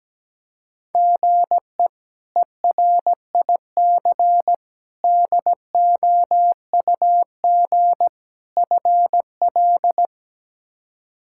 Morse code, heard as a keyed tone, about 17 wpm, 700 Hz. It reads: GE ERIC DOUG FL
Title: K1USN SST QSO Practice - 17wpm